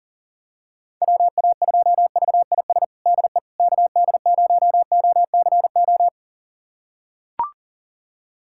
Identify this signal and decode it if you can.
Morse code, heard as a keyed tone, about 40 wpm, 700 Hz. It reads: WA2VIS DE KD0OCO
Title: USA Callsign DE Callsign - 6 Characters - 40wpm